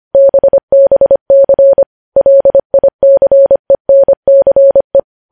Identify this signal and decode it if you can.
Morse code, heard as a keyed tone, about 25 wpm, 550 Hz. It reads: BBC LICENCE
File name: podcast / News headlines-1x-25x25